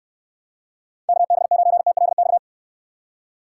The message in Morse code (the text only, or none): KB9TLX